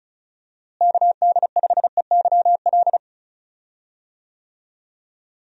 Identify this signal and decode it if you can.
Morse code, heard as a keyed tone, about 34 wpm, 700 Hz. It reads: KD5EYL
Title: USA Call Signs - 6 Characters - Repeated 3x - 35wpm